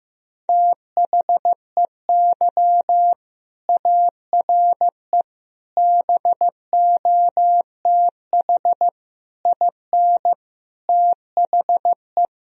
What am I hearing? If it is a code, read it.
Morse code, heard as a keyed tone, about 16 wpm, 700 Hz. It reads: THEY ARE BOTH IN THE
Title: Sentences from Top 1000 Words - Review Part 1 - 15wpm